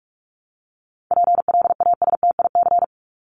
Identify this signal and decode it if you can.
Morse code, heard as a keyed tone, about 44 wpm, 700 Hz. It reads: PLASTIC
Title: Top 1001-1250 Words 45wpm